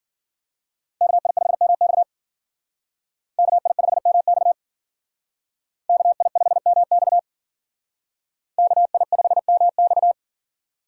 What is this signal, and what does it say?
Morse code, heard as a keyed tone, about 51 wpm, 700 Hz. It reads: KI5MX KI5MX KI5MX KI5MX